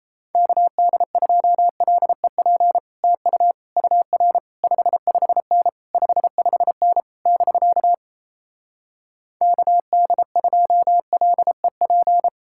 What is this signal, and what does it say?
Morse code, heard as a keyed tone, about 37 wpm, 700 Hz. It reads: KD2LEP TU UR 55N 55N <BK> KD2LEP